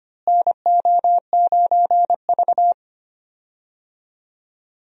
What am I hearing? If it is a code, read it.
Morse code, heard as a keyed tone, about 25 wpm, 700 Hz. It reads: NO9V
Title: USA Call Signs - 4 Characters - Repeated 2x - 25wpm